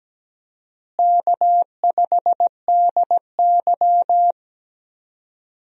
Morse code, heard as a keyed tone, about 17 wpm, 700 Hz. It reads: K5DY
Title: USA Call Signs - 4 Characters - Repeated 3x - 17wpm